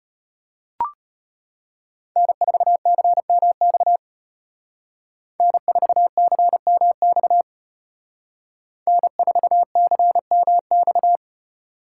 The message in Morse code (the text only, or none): N4CMX N4CMX N4CMX